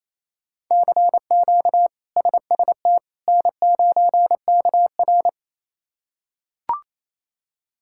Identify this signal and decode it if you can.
Morse code, heard as a keyed tone, about 28 wpm, 700 Hz. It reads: CQ SST N9KR
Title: K1USN SST QSO Practice - 28wpm